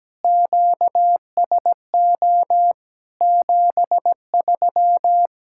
Morse code, heard as a keyed tone, about 17 wpm, 700 Hz. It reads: QSO 73
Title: Practice QSO - Exchange 4 - Ending - 17wpm